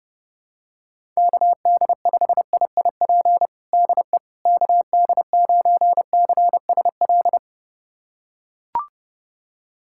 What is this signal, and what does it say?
Morse code, heard as a keyed tone, about 30 wpm, 700 Hz. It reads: KD5IIP DE KD9CSL
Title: USA Callsign DE Callsign - 6 Characters - 30wpm